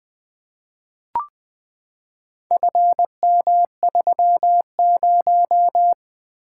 Morse code, heard as a keyed tone, about 20 wpm, 700 Hz. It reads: FM30
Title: Four Letter-Number Combinations - 20wpm